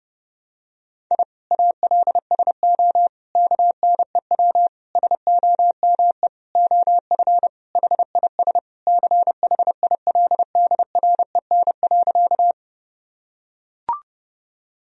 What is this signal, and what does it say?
Morse code, heard as a keyed tone, about 30 wpm, 700 Hz. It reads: I ALSO KNEW SOME OF HIS CHILDREN.
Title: Sentences from Top 1000 Words - Review Part 1 - 30wpm